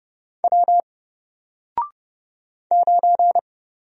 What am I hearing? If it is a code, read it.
Morse code, heard as a keyed tone, about 30 wpm, 700 Hz. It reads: W 9
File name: Single-Letter-Number-30wpm